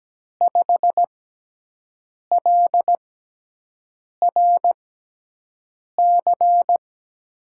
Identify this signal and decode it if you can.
Morse code, heard as a keyed tone, about 17 wpm, spaced as for 6 wpm, 700 Hz. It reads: 5LRC
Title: USA Call Signs - 6 Characters - 6x Character Spacing - 17wpm